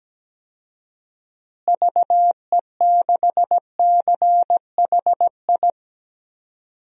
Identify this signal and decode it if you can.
Morse code, heard as a keyed tone, about 17 wpm, 700 Hz. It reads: VE6CHI
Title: DXCC-100 Call Signs - 6 Characters - Repeated 2x - 17wpm